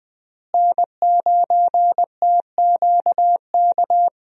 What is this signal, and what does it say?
Morse code, heard as a keyed tone, about 21 wpm, 700 Hz. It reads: N9TQK
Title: USA Call Signs - 5 Characters - Repeated 3x - 20wpm